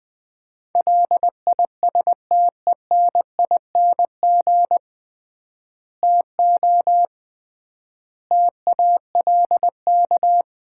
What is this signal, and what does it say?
Morse code, heard as a keyed tone, about 20 wpm, 700 Hz. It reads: LISTENING TO TALK
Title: 3x Word-Spacing - Sets of 5 Words - 20wpm